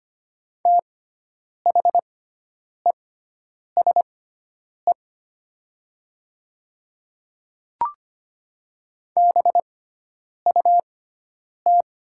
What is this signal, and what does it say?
Morse code, heard as a keyed tone, about 25 wpm, spaced as for 9 wpm, 700 Hz. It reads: THESE BUT
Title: 6x Character Spacing - Top 300 Words - 25wpm